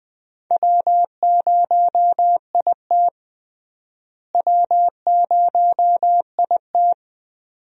Morse code, heard as a keyed tone, about 20 wpm, 700 Hz. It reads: W0IT W0IT
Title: USA Call Signs - 4 Characters - Repeated 2x - 20wpm